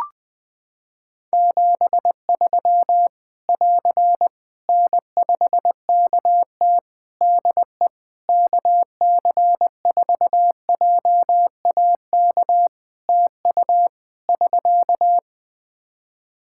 Morse code, heard as a keyed tone, about 21 wpm, 700 Hz. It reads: 73 <AR> N5KT DE KC4JAK TU <SK>
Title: K1USN SST QSO Practice 20wpm